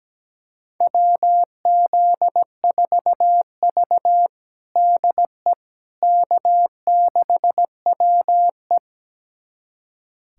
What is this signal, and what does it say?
Morse code, heard as a keyed tone, about 17 wpm, 700 Hz. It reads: WZ4V DE K6WE
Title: USA Callsign DE Callsign - 4 Characters - 17wpm